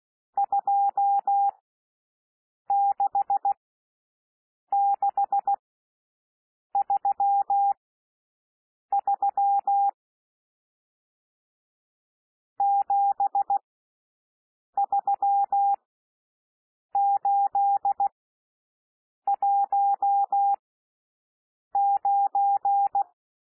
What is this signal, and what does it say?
Morse code, heard as a keyed tone, about 16 wpm, spaced as for 6 wpm, 800 Hz. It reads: 26633 73819
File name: SOUND NUMBERS 16 wpm